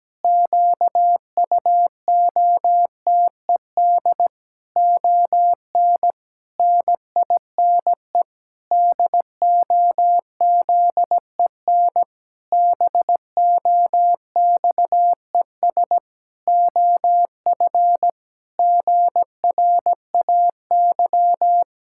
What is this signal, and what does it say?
Morse code, heard as a keyed tone, about 17 wpm, 700 Hz. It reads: QUOTED ON NINE DOZEN BOXES OF GRAY
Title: Pangram Challenge v2 - 17wpm